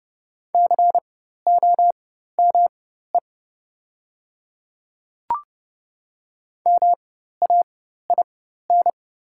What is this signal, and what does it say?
Morse code, heard as a keyed tone, about 31 wpm, spaced as for 14 wpm, 700 Hz. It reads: COME MAIN
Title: 4x Character Spacing - Top 300 Words - 30wpm